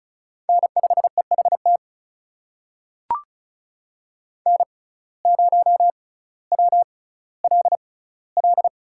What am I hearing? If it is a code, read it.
Morse code, heard as a keyed tone, about 36 wpm, spaced as for 18 wpm, 700 Hz. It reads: N5EHT N0WLL
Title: USA Call Signs - 5 Characters - 6x Character Spacing - 35wpm